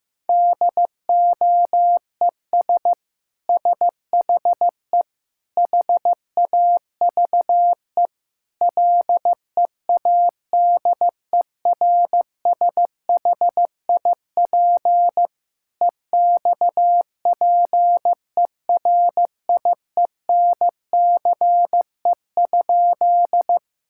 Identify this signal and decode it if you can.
Morse code, heard as a keyed tone, about 16 wpm, 700 Hz. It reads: DOES SHE HAVE LEADERSHIP EXPERIENCE?
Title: Sentences from Top 1600 Words 15wpm